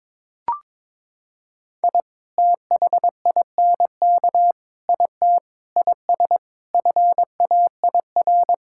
Morse code, heard as a keyed tone, about 22 wpm, 700 Hz. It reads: I THINK IT IS FAIR
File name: practice / 22wpm / Sets of 8 Words - 22wpm